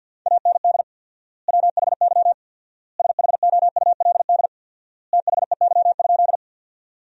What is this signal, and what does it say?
Morse code, heard as a keyed tone, about 53 wpm, 700 Hz. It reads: AND WHY SHOULD THEY?
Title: 4x Word-Spacing - Sentences from Top 300 Words - 50wpm